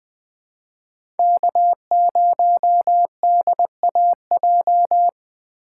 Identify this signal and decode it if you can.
Morse code, heard as a keyed tone, about 20 wpm, 700 Hz. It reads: K0DAJ
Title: Speed Racing - USA 1x3 Call Signs 17wpm